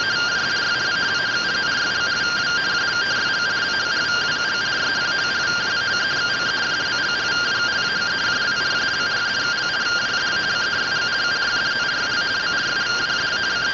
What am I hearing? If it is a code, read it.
Two-tone teleprinter signal, about 170 Hz shift.
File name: findings / 100 surview 0.gen-08 rtty rx